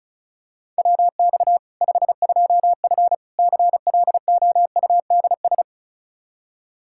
Morse code, heard as a keyed tone, about 36 wpm, 700 Hz. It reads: WX 52F CLOUDS
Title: K1USN SST QSO Practice 35wpm